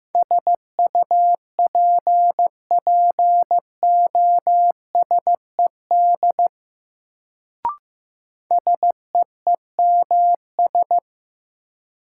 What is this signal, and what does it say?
Morse code, heard as a keyed tone, about 15 wpm, 700 Hz. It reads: SUPPOSED SEEMS
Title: Sets of 3 Words - New Word Drill - 15wpm